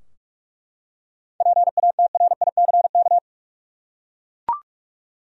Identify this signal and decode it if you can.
Morse code, heard as a keyed tone, about 42 wpm, 700 Hz. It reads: PATRICK
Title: USA Common Names - 45wpm